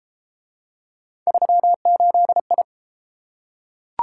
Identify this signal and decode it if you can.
Morse code, heard as a keyed tone, about 29 wpm, 700 Hz. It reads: 38I